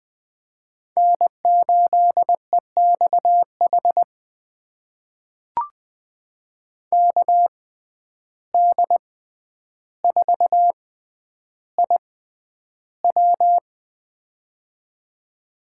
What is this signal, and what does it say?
Morse code, heard as a keyed tone, about 20 wpm, spaced as for 10 wpm, 700 Hz. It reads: N8EXH KD4IW
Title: USA Call Signs - 5 Characters - 6x Character Spacing - 20wpm